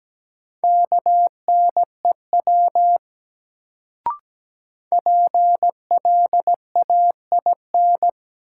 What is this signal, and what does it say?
Morse code, heard as a keyed tone, about 17 wpm, 700 Hz. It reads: KNEW PLAIN
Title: Sentences from Top 400 Words - New Word Drill - 17wpm